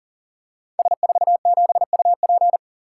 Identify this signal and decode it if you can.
Morse code, heard as a keyed tone, about 38 wpm, 690 Hz. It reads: S47UP
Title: Code Groups - 5 Letter:Numbers - 40wpm